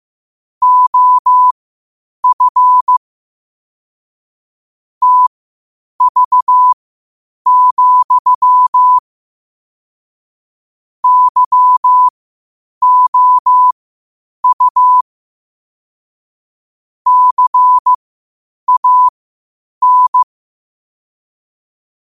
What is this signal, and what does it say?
Morse code, heard as a keyed tone, about 15 wpm, spaced as for 8 wpm, 1000 Hz. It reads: OF TV, YOU CAN
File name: qotdfiles / 8wpm